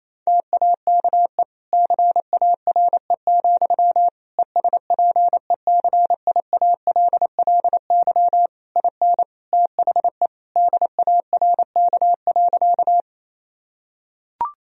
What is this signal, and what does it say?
Morse code, heard as a keyed tone, about 28 wpm, 700 Hz. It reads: TAKE CARE, ESPECIALLY IN THE DARK.
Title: Sentences from Top 1000 Words - Review Part 4 - 28wpm